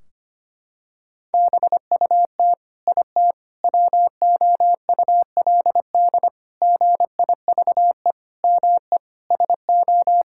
Morse code, heard as a keyed tone, about 25 wpm, 700 Hz. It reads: BUT IT WOULD GIVE ME SO
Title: Sentences from Top 1600 Words 25wpm